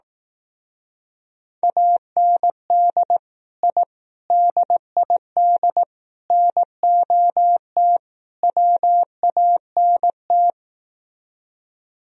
Morse code, heard as a keyed tone, about 18 wpm, 700 Hz. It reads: AND I DID NOT WANT
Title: Speed-Racing - Sets of 5 Words 15wpm